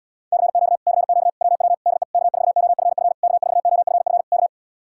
Morse code, heard as a keyed tone, about 77 wpm, 700 Hz. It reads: CQ CQ WW DE K4OFF K4OFF K